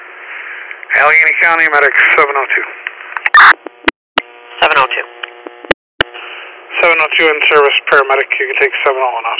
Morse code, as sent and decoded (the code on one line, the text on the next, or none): none